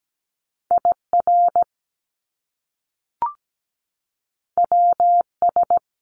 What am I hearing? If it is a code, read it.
Morse code, heard as a keyed tone, about 17 wpm, 700 Hz. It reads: IR WS
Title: Top Two Letter Combinations in Words 17wpm